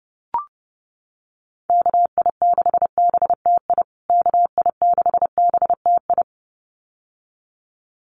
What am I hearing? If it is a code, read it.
Morse code, heard as a keyed tone, about 30 wpm, 700 Hz. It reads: KI6BTI KI6BTI
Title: USA Call Signs Repeated 2x 30wpm